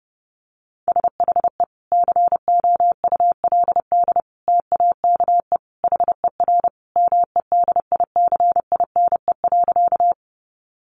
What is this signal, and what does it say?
Morse code, heard as a keyed tone, about 30 wpm, 700 Hz. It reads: SHE COULD TAKE HER MEDICINE.
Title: Sentences from Top 900 Words 30wpm